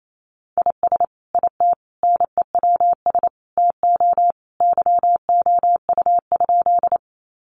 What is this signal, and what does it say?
Morse code, heard as a keyed tone, about 28 wpm, 700 Hz. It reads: IS IT NEWS TO YOU?
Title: Sentences from Top 800 Words 28wpm